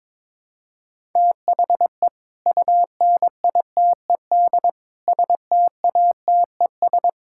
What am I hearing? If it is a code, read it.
Morse code, heard as a keyed tone, about 22 wpm, 700 Hz. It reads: THE UNITED STATES
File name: practice / 22wpm / Build A Phrase - 5-Word Phrases - 22wpm